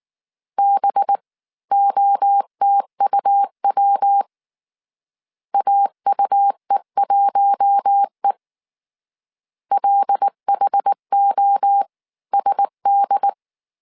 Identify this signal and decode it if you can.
Morse code, heard as a keyed tone, about 19 wpm, 800 Hz. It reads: BOTUW AUE1E LHOSD
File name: CW / DET / 1 6 1500C S C 18wpm 2356